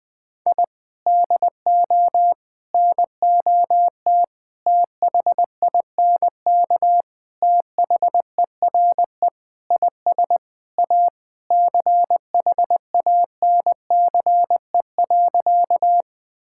Morse code, heard as a keyed tone, about 20 wpm, 700 Hz. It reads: I DO NOT THINK THERE IS A CHANCE.
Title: Sentences with 5-Word Phrases - 20wpm